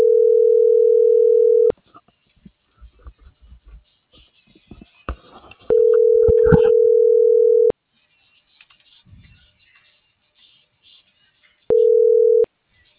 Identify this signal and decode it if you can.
Ringback tone (North America), 440 + 480 Hz.